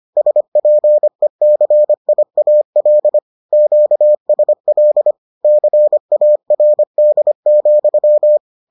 Morse code, heard as a keyed tone, about 25 wpm, 600 Hz. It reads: SPECIAL QSL CARD,